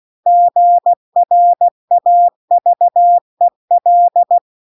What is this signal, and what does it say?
Morse code, heard as a keyed tone, about 16 wpm, 700 Hz. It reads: GRAVEL